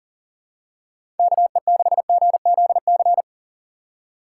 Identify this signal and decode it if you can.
Morse code, heard as a keyed tone, about 40 wpm, 700 Hz. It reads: KE6GZC
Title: USA Call Signs - 6 Characters - 4x Character Spacing - 40wpm